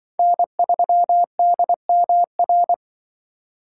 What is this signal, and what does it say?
Morse code, heard as a keyed tone, about 23 wpm, 700 Hz. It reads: N3DMR
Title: Speed Racing - USA 1x3 Call Signs 20wpm